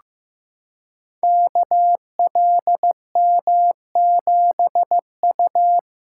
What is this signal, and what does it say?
Morse code, heard as a keyed tone, about 15 wpm, 700 Hz. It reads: KLM7U
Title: Code Groups - 5 Letter:Numbers - 15wpm